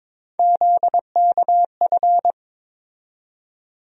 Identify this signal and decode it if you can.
Morse code, heard as a keyed tone, about 21 wpm, 700 Hz. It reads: ZKF
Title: USA Call Sign Postfixes - 3 Letters 22wpm